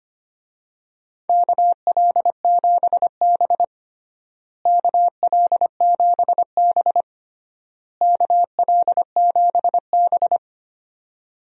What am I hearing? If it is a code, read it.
Morse code, heard as a keyed tone, about 25 wpm, 700 Hz. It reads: KL7B KL7B KL7B
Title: DXCC-100 Call Signs - 4 Characters - Repeated 3x - 25wpm